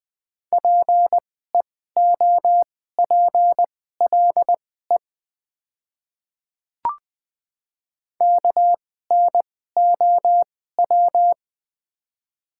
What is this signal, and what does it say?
Morse code, heard as a keyed tone, about 21 wpm, spaced as for 15 wpm, 700 Hz. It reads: PEOPLE KNOW
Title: 2x Character Spacing - Top 100 Words - 20wpm